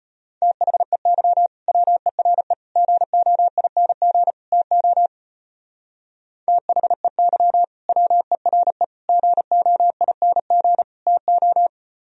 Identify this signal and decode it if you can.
Morse code, heard as a keyed tone, about 33 wpm, 700 Hz. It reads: THEY WERE GOING TO THEY WERE GOING TO